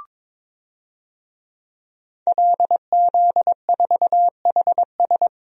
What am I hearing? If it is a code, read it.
Morse code, heard as a keyed tone, about 22 wpm, 700 Hz. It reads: LZ4HS
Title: DXCC-100 Call Signs - 5 Characters - Repeated 3x - 22wpm